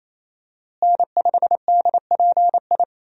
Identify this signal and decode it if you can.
Morse code, heard as a keyed tone, about 28 wpm, 700 Hz. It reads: N5DPI
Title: USA Call Signs - 5 Characters - Repeated 3x - 28wpm